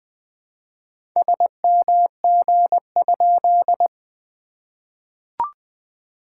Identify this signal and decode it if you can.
Morse code, heard as a keyed tone, about 20 wpm, 700 Hz. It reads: SMG?
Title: SOTA QSO Practice 20wpm